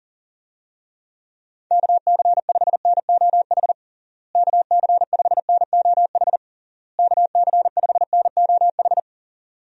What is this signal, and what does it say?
Morse code, heard as a keyed tone, about 37 wpm, 700 Hz. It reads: KC5NOH KC5NOH KC5NOH
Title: USA Call Signs - 6 Characters - Repeated 3x - 40wpm